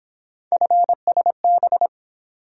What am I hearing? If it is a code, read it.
Morse code, heard as a keyed tone, about 26 wpm, 700 Hz. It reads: FSB
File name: Speed-Racing - Three Letter-Number Combinations - 22wpm